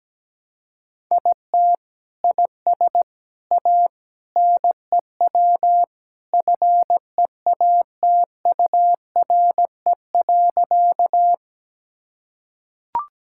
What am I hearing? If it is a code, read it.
Morse code, heard as a keyed tone, about 17 wpm, 700 Hz. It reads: IT IS A NEW FEATURE.
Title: Sentences from Top 1000 Words - Review Part 5 - 17wpm